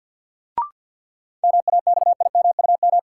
Morse code, heard as a keyed tone, about 53 wpm, 700 Hz. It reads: MAXIMUM
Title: Sentences from Top 1300 Words - New Word Drill - 50wpm